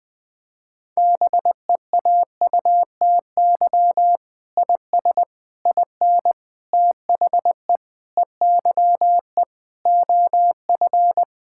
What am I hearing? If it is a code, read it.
Morse code, heard as a keyed tone, about 20 wpm, 700 Hz. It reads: BEAUTY IS IN THE EYE OF
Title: Sentences with 7-Word Phrases - 20wpm